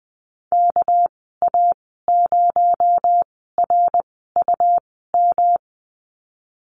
Morse code, heard as a keyed tone, about 20 wpm, spaced as for 15 wpm, 700 Hz. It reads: KA0RUM
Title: 238 Lesson 38g - Letter q - Call Signs 20wpm